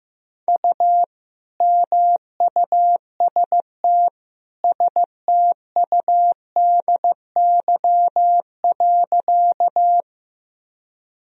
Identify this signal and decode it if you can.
Morse code, heard as a keyed tone, about 16 wpm, 700 Hz. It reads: U MUST STUDY.
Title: Sentences from Top 300 Words and Abbreviations 15wpm